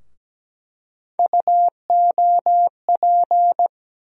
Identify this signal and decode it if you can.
Morse code, heard as a keyed tone, about 17 wpm, 700 Hz. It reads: UOP
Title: USA Call Sign Postfixes - 3 Letters 17wpm